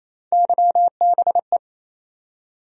Morse code, heard as a keyed tone, about 28 wpm, 700 Hz. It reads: YBE
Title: USA Call Sign Postfixes - 3 Letters 28wpm